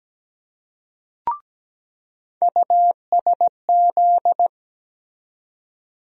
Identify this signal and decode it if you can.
Morse code, heard as a keyed tone, about 17 wpm, 700 Hz. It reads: USZ
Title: USA Call Sign Postfixes - 3 Letters 17wpm